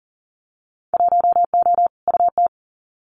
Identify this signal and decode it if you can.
Morse code, heard as a keyed tone, about 40 wpm, 700 Hz. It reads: 1O UT
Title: Winter Field Day Exchanges Encore 40wpm